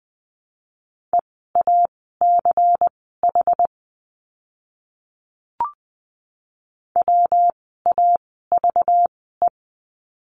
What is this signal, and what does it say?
Morse code, heard as a keyed tone, about 20 wpm, spaced as for 14 wpm, 700 Hz. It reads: EACH WAVE
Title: 192 Lesson 31f - Letter v - Words 20wpm